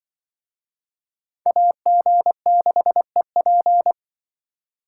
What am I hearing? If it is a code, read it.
Morse code, heard as a keyed tone, about 25 wpm, 700 Hz. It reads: AG6EP
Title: Speed Racing - USA 2x2 Call Signs 20wpm